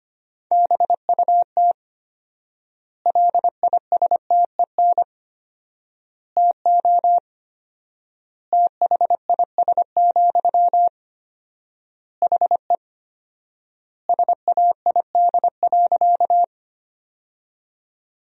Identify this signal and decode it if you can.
Morse code, heard as a keyed tone, about 25 wpm, 700 Hz. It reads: BUT LISTEN TO THIS, HE SAID.
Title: 4x Word-Spacing - Sentences from Top 400 Words - 25wpm